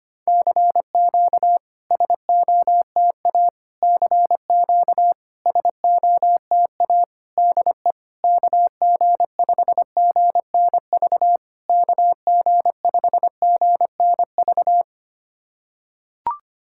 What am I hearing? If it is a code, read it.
Morse code, heard as a keyed tone, about 25 wpm, 690 Hz. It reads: CQ SOTA CQ SOTA DE KG5GNV KG5GNV
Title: SOTA QSO Practice 25wpm